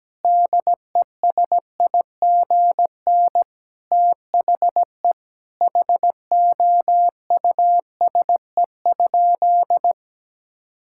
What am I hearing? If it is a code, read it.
Morse code, heard as a keyed tone, about 17 wpm, 700 Hz. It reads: DESIGN THE HOUSE?
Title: Sentences from Top 1000 Words - Review Part 5 - 17wpm